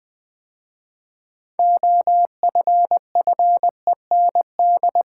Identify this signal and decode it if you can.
Morse code, heard as a keyed tone, about 20 wpm, 700 Hz. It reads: OFFEND
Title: 6 Letter Words - First 3 Letters Given - 20wpm